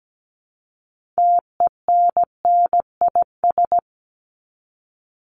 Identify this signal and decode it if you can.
Morse code, heard as a keyed tone, about 17 wpm, 700 Hz. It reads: TENNIS
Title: Top 1501-1750 Words 17wpm